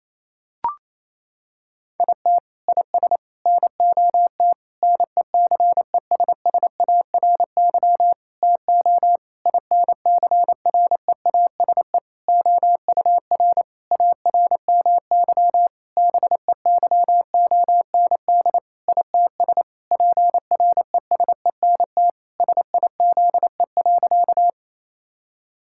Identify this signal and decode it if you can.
Morse code, heard as a keyed tone, about 28 wpm, 700 Hz. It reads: IT IS NOT NECESSARY TO INCREASE OUR ARMY BEYOND ITS PRESENT SIZE.
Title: Sentences from Top 1000 Words - Review Part 6 - 28wpm